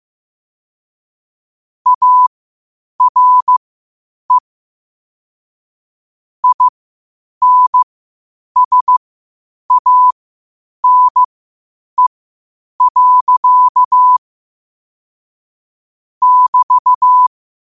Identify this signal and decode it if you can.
Morse code, heard as a keyed tone, about 15 wpm, spaced as for 8 wpm, 1000 Hz. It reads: ARE INSANE. <BT>
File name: qotdfiles / 8wpm